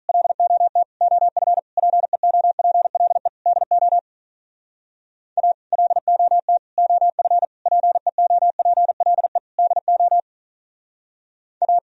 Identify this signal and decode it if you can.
Morse code, heard as a keyed tone, about 43 wpm, 700 Hz. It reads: LOT OF PEOPLE DO A LOT OF PEOPLE DO A
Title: Speed-Racing - Sets of 5 Words 35wpm